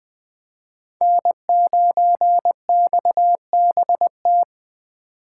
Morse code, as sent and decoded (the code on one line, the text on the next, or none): -. ----. -..- -... -
N9XBT